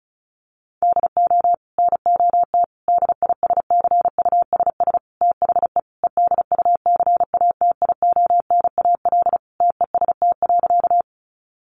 Morse code, heard as a keyed tone, about 34 wpm, 700 Hz. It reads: DO NOT DISCUSS THE EDUCATIONAL TEST.
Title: Sentences from Top 1200 Words 35wpm